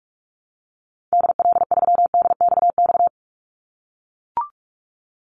Morse code, heard as a keyed tone, about 44 wpm, 700 Hz. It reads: DL3DXX
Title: Call Signs - 45wpm